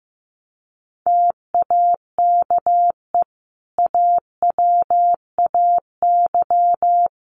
Morse code, heard as a keyed tone, about 15 wpm, 700 Hz. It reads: TAKE AWAY
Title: Sentences from Top 1300 Words 15wpm